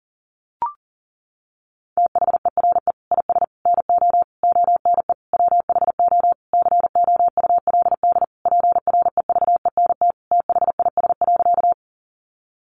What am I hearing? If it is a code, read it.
Morse code, heard as a keyed tone, about 40 wpm, 700 Hz. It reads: THERE IS NO ONE WHO COULD PREVENT THIS.
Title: Sentences from Top 1100 Words 40wpm